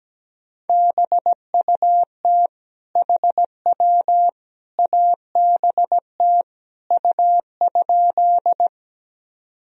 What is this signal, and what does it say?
Morse code, heard as a keyed tone, about 17 wpm, 700 Hz. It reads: BUT HW ABT U?
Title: Sentences from Top 100 Words and Abbreviations 17wpm